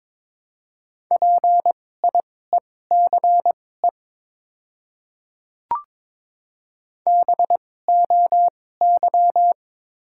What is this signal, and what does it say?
Morse code, heard as a keyed tone, about 23 wpm, spaced as for 16 wpm, 700 Hz. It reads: PIECE BOY
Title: 2x Character Spacing - Top 500 Words - 22wpm